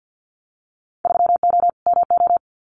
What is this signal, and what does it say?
Morse code, heard as a keyed tone, about 50 wpm, 700 Hz. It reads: 3O MO
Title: Winter Field Day Exchanges Encore 50wpm